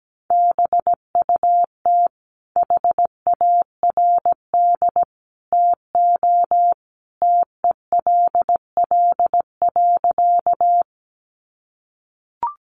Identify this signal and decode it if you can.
Morse code, heard as a keyed tone, about 17 wpm, 700 Hz. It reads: BUT HARD TO TELL.
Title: Sentences from Top 300 Words 17wpm